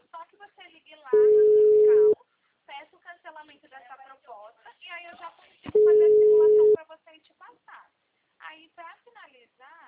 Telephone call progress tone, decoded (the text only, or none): ringback tone